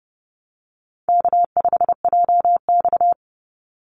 Morse code, heard as a keyed tone, about 30 wpm, 700 Hz. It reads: K5JX
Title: Easy Call Signs - 30wpm